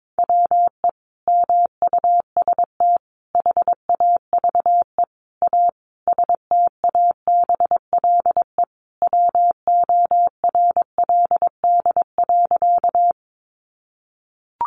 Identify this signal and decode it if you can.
Morse code, heard as a keyed tone, about 22 wpm, 700 Hz. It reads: WE MUST HAVE A STABLE WORLD.
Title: Sentences from Top 1500 Words 22wpm